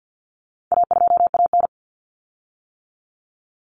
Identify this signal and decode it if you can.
Morse code, heard as a keyed tone, about 50 wpm, 700 Hz. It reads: A2AN
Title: Common Cut Numbers - 4-Digit 50wpm